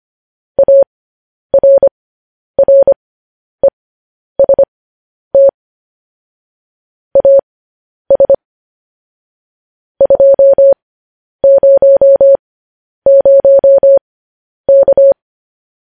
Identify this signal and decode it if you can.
Morse code, heard as a keyed tone, about 25 wpm, spaced as for 10 wpm, 550 Hz. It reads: ARREST AS 200K
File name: podcast / News headlines-1x-25x10